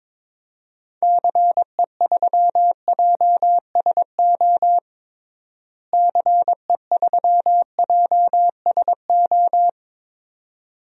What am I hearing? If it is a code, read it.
Morse code, heard as a keyed tone, about 22 wpm, 700 Hz. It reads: CE3JSO CE3JSO
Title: DXCC-100 Call Signs - 6 Characters - Repeated 2x - 22wpm